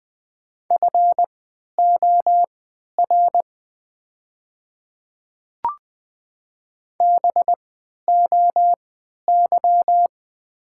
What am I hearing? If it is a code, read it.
Morse code, heard as a keyed tone, about 20 wpm, spaced as for 11 wpm, 700 Hz. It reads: FOR BOY
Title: 3x Character Spacing - Top 500 Words - 20wpm